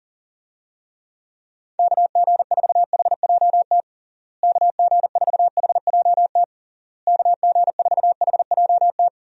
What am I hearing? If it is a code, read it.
Morse code, heard as a keyed tone, about 40 wpm, 700 Hz. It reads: KG4HJT KG4HJT KG4HJT
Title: USA Call Signs - 6 Characters - Repeated 3x - 40wpm